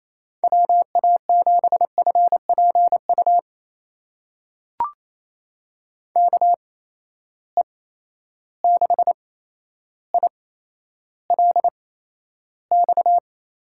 Morse code, heard as a keyed tone, about 28 wpm, spaced as for 12 wpm, 700 Hz. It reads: WA7FPU KE6ILX
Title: USA Call Signs - 6 Characters - 8x Character Spacing - 28wpm